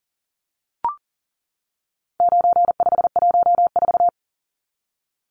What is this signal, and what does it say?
Morse code, heard as a keyed tone, about 40 wpm, 700 Hz. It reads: 9514
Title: Numbers - 4-Digit 40wpm